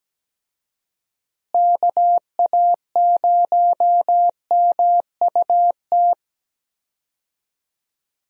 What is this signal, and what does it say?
Morse code, heard as a keyed tone, about 17 wpm, 700 Hz. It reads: KA0MUT
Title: USA Call Signs - 6 Characters - Repeated 3x - 17wpm